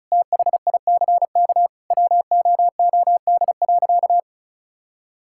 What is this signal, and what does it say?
Morse code, heard as a keyed tone, about 36 wpm, 700 Hz. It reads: THICK WOOD.
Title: Sentences from Top 1600 Words 35wpm